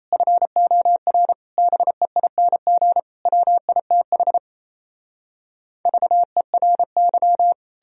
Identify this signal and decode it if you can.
Morse code, heard as a keyed tone, about 38 wpm, 700 Hz. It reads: FOR BEING WITH VERY